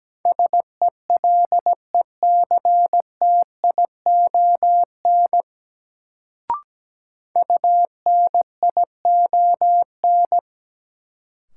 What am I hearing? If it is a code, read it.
Morse code, heard as a keyed tone, about 17 wpm, 700 Hz. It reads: SELECTION UNION
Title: Sentences from Top 1400 Words - New Word Drill - 17wpm